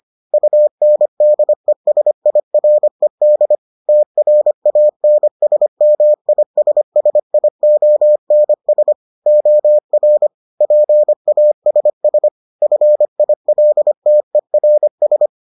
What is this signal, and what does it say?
Morse code, heard as a keyed tone, about 25 wpm, 600 Hz. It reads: UNDESIRED TRANSMISSIONS OR PASS FILTERS